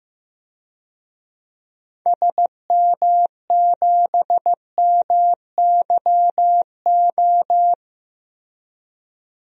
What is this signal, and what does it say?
Morse code, heard as a keyed tone, about 15 wpm, 700 Hz. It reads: SM7MYO
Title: DXCC-100 Call Signs - 6 Characters - Repeated 2x - 15wpm